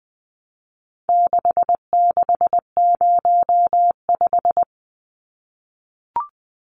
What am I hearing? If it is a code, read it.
Morse code, heard as a keyed tone, about 20 wpm, 700 Hz. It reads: 6605
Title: Numbers - 4-Digit 20wpm